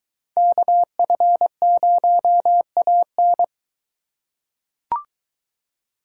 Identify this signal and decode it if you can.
Morse code, heard as a keyed tone, about 22 wpm, 700 Hz. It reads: KF0AN